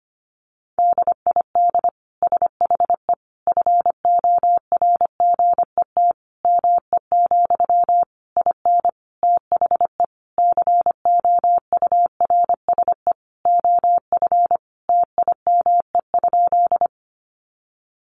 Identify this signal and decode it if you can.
Morse code, heard as a keyed tone, about 25 wpm, 700 Hz. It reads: DID SHE FORGET ME, IN THE COURSE OF TIME?
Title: Sentences from Top 1200 Words 25wpm